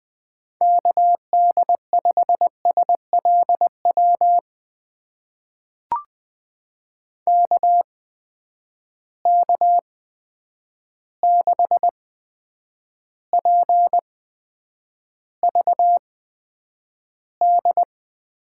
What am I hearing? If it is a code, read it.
Morse code, heard as a keyed tone, about 20 wpm, spaced as for 9 wpm, 700 Hz. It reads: KD5SLW KK6PVD
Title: USA Call Signs - 6 Characters - 8x Character Spacing - 20wpm